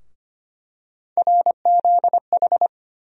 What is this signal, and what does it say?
Morse code, heard as a keyed tone, about 25 wpm, 700 Hz. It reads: RZH
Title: USA Call Sign Postfixes - 3 Letters 25wpm